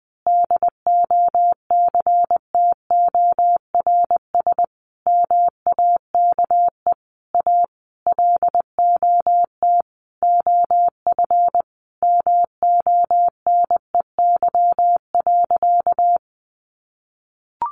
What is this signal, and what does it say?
Morse code, heard as a keyed tone, about 20 wpm, 700 Hz. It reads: DOCTORS MAKE A LOT OF MONEY.
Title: Easy Phrases 20wpm